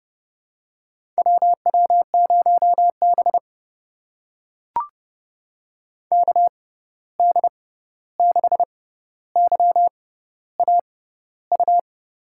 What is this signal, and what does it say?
Morse code, heard as a keyed tone, about 30 wpm, spaced as for 14 wpm, 700 Hz. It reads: WW0B KD6YAU
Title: USA Call Signs - 6x Character Spacing - 30wpm